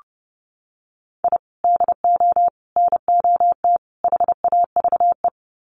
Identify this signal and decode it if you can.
Morse code, heard as a keyed tone, about 30 wpm, 700 Hz. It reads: I DO NOT HAVE
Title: Sets of 4 Words 30wpm